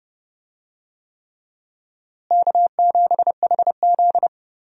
Morse code, heard as a keyed tone, about 29 wpm, 700 Hz. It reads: K7HZ
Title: USA Call Signs - 4 Characters - Repeated 3x - 30wpm